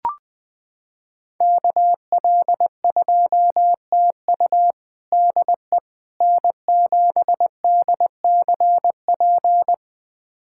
Morse code, heard as a keyed tone, about 20 wpm, 700 Hz. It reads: KL2TU DE N7DCP
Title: USA Callsign DE Callsign - 5 Characters - 20wpm